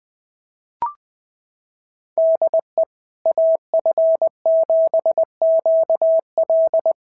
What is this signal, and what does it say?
Morse code, heard as a keyed tone, about 20 wpm, 660 Hz. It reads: DE AF7QL
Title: SOTA QSO Practice 20wpm